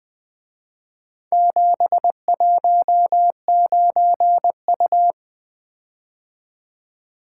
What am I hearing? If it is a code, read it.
Morse code, heard as a keyed tone, about 24 wpm, 700 Hz. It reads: 719U
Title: Speed-Racing - Four Letter-Number Combinations - 20wpm